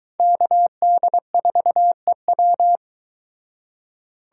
Morse code, heard as a keyed tone, about 23 wpm, 700 Hz. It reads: KD4EW